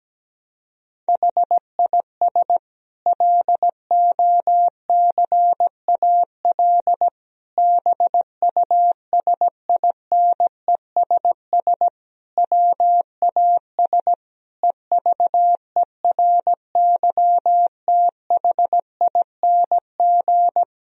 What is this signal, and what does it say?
Morse code, heard as a keyed tone, about 17 wpm, 700 Hz. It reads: HIS LOCAL BUSINESS WAS EVERYTHING
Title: Sentences from Top 1000 Words - Review Part 3 - 17wpm